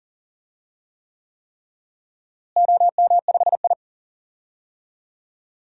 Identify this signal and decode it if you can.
Morse code, heard as a keyed tone, about 40 wpm, 700 Hz. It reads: OM5I
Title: DXCC-100 Call Signs - 4 Characters - Repeated 3x - 40wpm